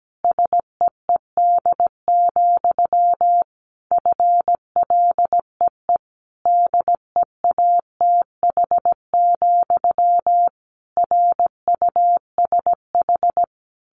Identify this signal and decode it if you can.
Morse code, heard as a keyed tone, about 17 wpm, 700 Hz. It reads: SEED, FLEE DEATH, RUSH
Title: A Sound Of Thunder 17wpm